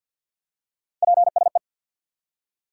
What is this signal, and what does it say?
Morse code, heard as a keyed tone, about 48 wpm, 700 Hz. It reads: PSE
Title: Top 90 QSO Elements 50wpm